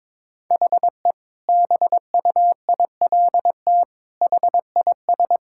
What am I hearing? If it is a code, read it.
Morse code, heard as a keyed tone, about 22 wpm, 700 Hz. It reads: HE BUILT HIS
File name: practice / 22wpm / Sentences with 2-Word Phrases - 22wpm